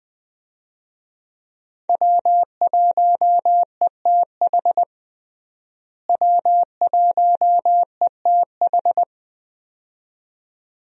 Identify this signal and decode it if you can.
Morse code, heard as a keyed tone, about 20 wpm, 700 Hz. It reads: W1ETH W1ETH
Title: USA Call Signs - 5 Characters - Repeated 2x - 20wpm